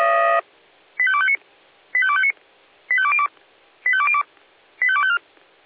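Five-tone selective call (ZVEI), heard as quick runs of five tones, tones of about 70 ms. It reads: 85259 85259 85202 85202 85254
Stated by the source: ZVEI five-tone call received on SDR.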